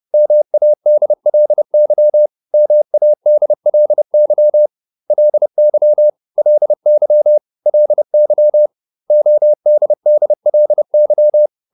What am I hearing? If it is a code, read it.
Morse code, heard as a keyed tone, about 30 wpm, 600 Hz. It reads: MADLY MADLY LY LY LY ODDLY